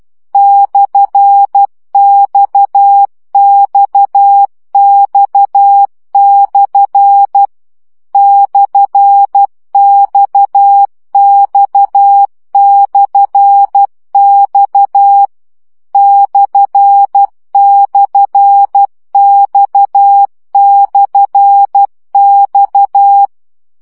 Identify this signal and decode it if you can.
Morse code, heard as a keyed tone, about 12 wpm, 800 Hz. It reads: /XXX/ /XX/X //X/X
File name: CW gyakorlas x per 10 baud